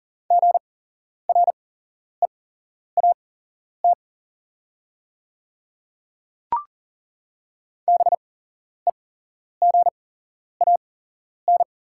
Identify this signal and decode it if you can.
Morse code, heard as a keyed tone, about 40 wpm, spaced as for 11 wpm, 700 Hz. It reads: GREAT BEGAN
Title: 8x Character Spacing - Top 300 Words - 40wpm